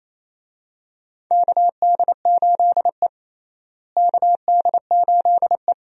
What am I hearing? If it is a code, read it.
Morse code, heard as a keyed tone, about 27 wpm, 700 Hz. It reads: KD8E KD8E
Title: USA Call Signs - 4 Characters - Repeated 2x - 28wpm